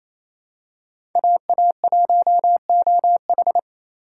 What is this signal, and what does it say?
Morse code, heard as a keyed tone, about 28 wpm, 700 Hz. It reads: AA1OH
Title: USA Call Signs - 5 Characters - 4x Character Spacing - 28wpm